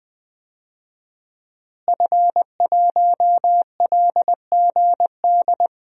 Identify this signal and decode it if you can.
Morse code, heard as a keyed tone, about 20 wpm, 700 Hz. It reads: F1LGD
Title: DXCC-100 Call Signs - 5 Characters - Repeated 3x - 20wpm